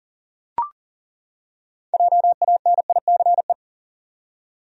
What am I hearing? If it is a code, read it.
Morse code, heard as a keyed tone, about 40 wpm, 700 Hz. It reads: JANICE
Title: USA Common Names - 40wpm